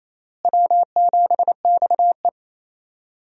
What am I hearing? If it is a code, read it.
Morse code, heard as a keyed tone, about 27 wpm, 700 Hz. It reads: W7XE
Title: USA Call Signs - 4 Characters - Repeated 3x - 28wpm